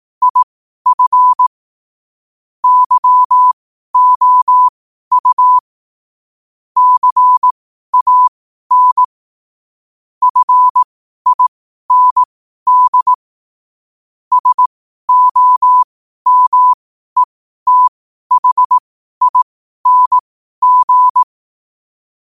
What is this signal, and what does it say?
Morse code, heard as a keyed tone, about 18 wpm, spaced as for 12 wpm, 1000 Hz. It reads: IF YOU CAN FIND SOMETHING
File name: qotdfiles / 12wpm